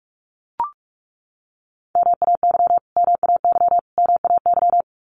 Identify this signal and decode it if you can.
Morse code, heard as a keyed tone, about 44 wpm, 700 Hz. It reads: MAY MAY MAY
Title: Top 200 Words Repeated 45wpm